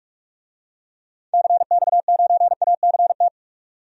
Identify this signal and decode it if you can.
Morse code, heard as a keyed tone, about 41 wpm, 700 Hz. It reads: CX9ACT
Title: DXCC-100 Call Signs - 6 Characters - 45wpm